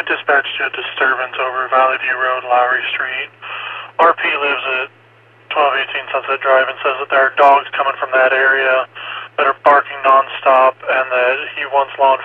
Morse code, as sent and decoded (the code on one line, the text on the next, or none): none